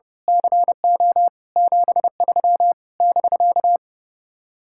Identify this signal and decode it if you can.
Morse code, heard as a keyed tone, about 29 wpm, 700 Hz. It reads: CO 73 <BK>
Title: Speed-Racing Common QSO Phrases Encore - 30wpm